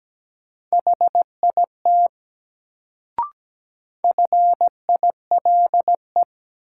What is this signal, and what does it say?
Morse code, heard as a keyed tone, about 17 wpm, 700 Hz. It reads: HIT FILE
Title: Sentences from Top 1200 Words - New Word Drill Alt - 17wpm